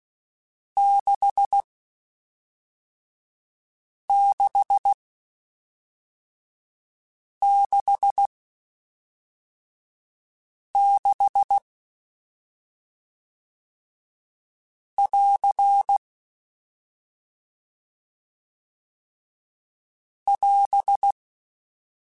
Morse code, heard as a keyed tone, about 16 wpm, spaced as for 3 wpm, 800 Hz. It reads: 6666 <AR> <AS>
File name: Short Runs / Short Runs-mp3s / Staccato@5